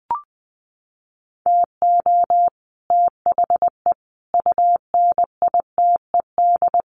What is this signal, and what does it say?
Morse code, heard as a keyed tone, about 20 wpm, 700 Hz. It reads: TO THE UNITED
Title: Sets of 5 Words Encore 20wpm